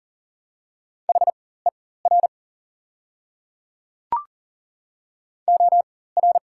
Morse code, heard as a keyed tone, about 40 wpm, spaced as for 19 wpm, 690 Hz. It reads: HER OR
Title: 4x Character Spacing - Top 300 Words - 40wpm